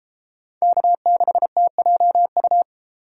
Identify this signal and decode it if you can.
Morse code, heard as a keyed tone, about 32 wpm, 700 Hz. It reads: K6TJU